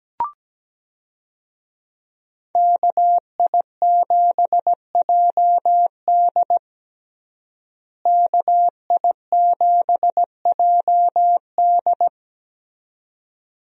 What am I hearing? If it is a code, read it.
Morse code, heard as a keyed tone, about 17 wpm, 700 Hz. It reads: KI7JD KI7JD
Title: USA Call Signs - 5 Characters - Repeated 2x - 17wpm